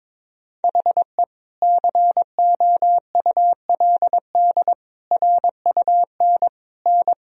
Morse code, heard as a keyed tone, about 21 wpm, 700 Hz. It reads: HE COULD RUN N
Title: Sentences from Top 300 Words and Abbreviations 22wpm